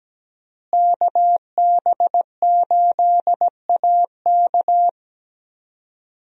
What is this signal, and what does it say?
Morse code, heard as a keyed tone, about 17 wpm, 700 Hz. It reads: KB8AK
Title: USA Call Signs - 5 Characters - Repeated 3x - 17wpm